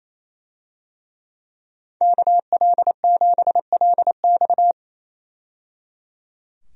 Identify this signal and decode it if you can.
Morse code, heard as a keyed tone, about 28 wpm, 700 Hz. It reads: KL7LX
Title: DXCC-100 Call Signs - 5 Characters - 28wpm